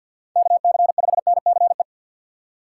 Morse code, heard as a keyed tone, about 53 wpm, 700 Hz. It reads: KC5NCE
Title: USA Call Signs - 6 Characters - 6x Character Spacing - 50wpm